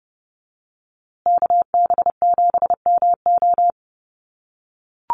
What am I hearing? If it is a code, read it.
Morse code, heard as a keyed tone, about 30 wpm, 700 Hz. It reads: KB7MO
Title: USA Call Signs Repeated 2x 30wpm